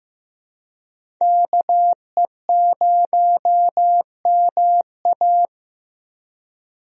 Morse code, heard as a keyed tone, about 15 wpm, 700 Hz. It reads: KE0MA
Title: USA Call Signs - 5 Characters - Repeated 3x - 15wpm